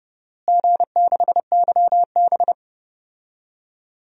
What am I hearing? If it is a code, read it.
Morse code, heard as a keyed tone, about 30 wpm, 700 Hz. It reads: G6YB
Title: DXCC-100 Call Signs - 4 Characters - Repeated 3x - 30wpm